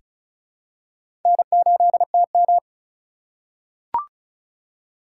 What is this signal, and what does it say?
Morse code, heard as a keyed tone, about 36 wpm, 700 Hz. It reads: N8TM
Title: USA Call Signs - 2x Character Spacing - 35wpm